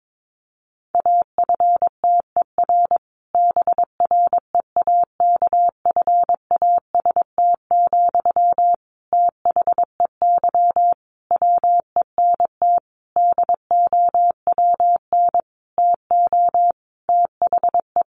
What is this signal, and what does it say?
Morse code, heard as a keyed tone, about 22 wpm, 700 Hz. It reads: AFTER BREAKFAST, THEY WENT DOWN TO THE
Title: Sentences from Top 1500 Words 22wpm